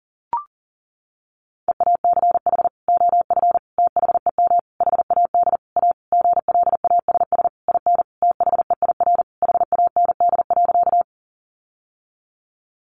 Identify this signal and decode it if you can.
Morse code, heard as a keyed tone, about 40 wpm, 700 Hz. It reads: EACH OF THEM HAD A GLASS IN THEIR HAND.
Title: Sentences from Top 1300 Words 40wpm